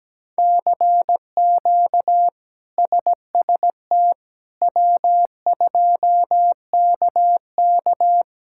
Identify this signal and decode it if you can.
Morse code, heard as a keyed tone, about 17 wpm, 700 Hz. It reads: CQ SST W2KK
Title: K1USN SST QSO Practice - 17wpm